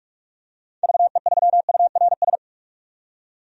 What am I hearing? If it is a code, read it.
Morse code, heard as a keyed tone, about 46 wpm, 700 Hz. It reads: VE3URS
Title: DXCC-100 Call Signs - 6 Characters - 45wpm